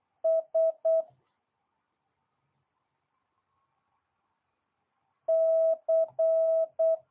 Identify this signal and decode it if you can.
Morse code, heard as a keyed tone, about 8 wpm, 650 Hz. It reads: S C
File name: jh8cbh / 2022cw